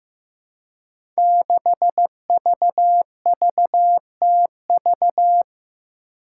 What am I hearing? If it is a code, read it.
Morse code, heard as a keyed tone, about 15 wpm, 700 Hz. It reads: 6VVTV
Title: Five Challenging Letter-Number Combinations 15wpm